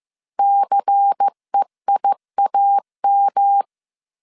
Morse code, heard as a keyed tone, about 15 wpm, 800 Hz. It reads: CEIAM